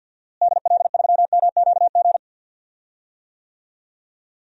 Morse code, heard as a keyed tone, about 46 wpm, 700 Hz. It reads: DL3MQG
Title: DXCC-100 Call Signs - 6 Characters - Repeated 3x - 50wpm